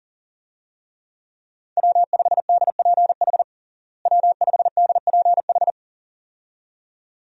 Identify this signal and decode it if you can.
Morse code, heard as a keyed tone, about 40 wpm, 700 Hz. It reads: W5DPH W5DPH
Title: USA Call Signs - 5 Characters - Repeated 2x - 40wpm